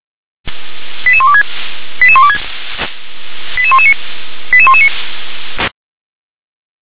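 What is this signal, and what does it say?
Five-tone selective call (ZVEI), heard as quick runs of five tones, tones of about 70 ms. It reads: 88126 88126 88119 88119
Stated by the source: ZVEI five-tone call received with an SDR.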